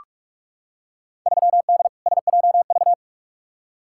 Morse code, heard as a keyed tone, about 42 wpm, 700 Hz. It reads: 3D SJV
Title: Field Day Exchanges - 45wpm